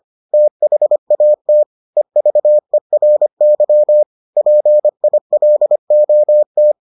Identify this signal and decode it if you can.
Morse code, heard as a keyed tone, about 25 wpm, 600 Hz. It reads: THAT EVERY PILOT